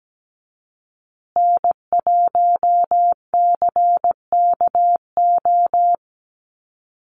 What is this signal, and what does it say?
Morse code, heard as a keyed tone, about 17 wpm, 700 Hz. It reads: N1CKO
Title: USA Call Signs Repeated 3x 17wpm